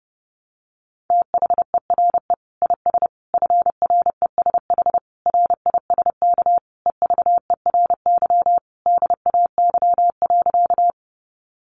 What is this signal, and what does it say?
Morse code, heard as a keyed tone, about 30 wpm, 700 Hz. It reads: THERE IS FRESH RISK EVERY DAY.
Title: Sentences from Top 1200 Words 30wpm